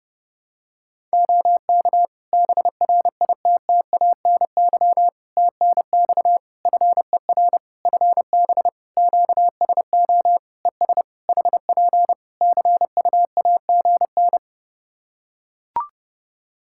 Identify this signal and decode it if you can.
Morse code, heard as a keyed tone, about 30 wpm, 700 Hz. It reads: OK BRITTANY TNX FER FB QSO ES HP CUAGN
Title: Practice QSO - Exchange 4 - Ending - 30wpm